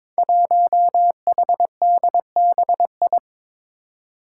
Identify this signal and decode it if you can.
Morse code, heard as a keyed tone, about 22 wpm, 700 Hz. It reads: 1HDBI
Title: Five Challenging Letter-Number Combinations 22wpm